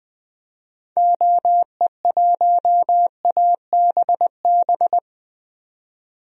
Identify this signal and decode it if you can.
Morse code, heard as a keyed tone, about 20 wpm, 700 Hz. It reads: OE1ABB
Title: DXCC-100 Call Signs - 6 Characters - 20wpm